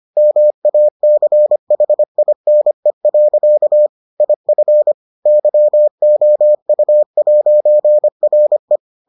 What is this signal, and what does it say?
Morse code, heard as a keyed tone, about 25 wpm, 600 Hz. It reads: MACHINE. IF YOU'RE